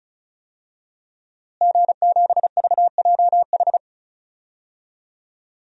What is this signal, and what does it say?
Morse code, heard as a keyed tone, about 34 wpm, 700 Hz. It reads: G7VJH
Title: DXCC-100 Call Signs - 5 Characters - Repeated 3x - 35wpm